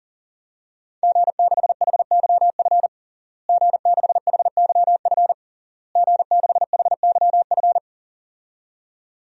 Morse code, heard as a keyed tone, about 40 wpm, 700 Hz. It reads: G6HYF G6HYF G6HYF
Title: DXCC-100 Call Signs - 5 Characters - Repeated 3x - 40wpm